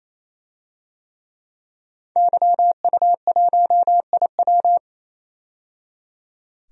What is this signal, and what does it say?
Morse code, heard as a keyed tone, about 28 wpm, 700 Hz. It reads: YU1IW
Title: DXCC-100 Call Signs - 5 Characters - 28wpm